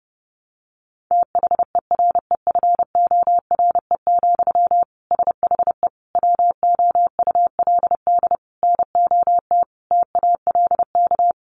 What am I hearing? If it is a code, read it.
Morse code, heard as a keyed tone, about 28 wpm, 700 Hz. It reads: THEREFORE, SHE WOULD NOT TALK
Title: Sentences from Top 800 Words 30wpm